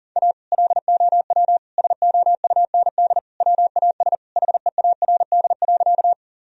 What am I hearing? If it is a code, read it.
Morse code, heard as a keyed tone, about 40 wpm, 700 Hz. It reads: A LOW SOUND WAS HEARD.
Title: Sentences from Top 1000 Words - Review Part 1 - 40wpm